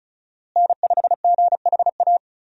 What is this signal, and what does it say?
Morse code, heard as a keyed tone, about 34 wpm, 700 Hz. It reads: N5GHA
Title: USA Call Signs - 5 Characters - Repeated 2x - 35wpm